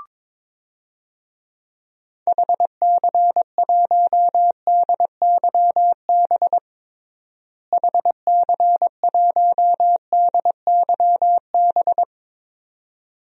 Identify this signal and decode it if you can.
Morse code, heard as a keyed tone, about 22 wpm, 700 Hz. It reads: HC1DYB HC1DYB
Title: DXCC-100 Call Signs - 6 Characters - Repeated 2x - 22wpm